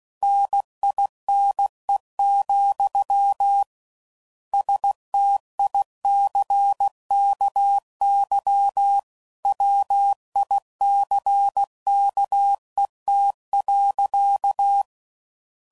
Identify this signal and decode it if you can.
Morse code, heard as a keyed tone, about 16 wpm, 800 Hz. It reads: NINE, STICKY WICKET.